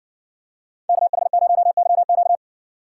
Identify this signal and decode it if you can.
Morse code, heard as a keyed tone, about 60 wpm, 700 Hz. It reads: KH0YQ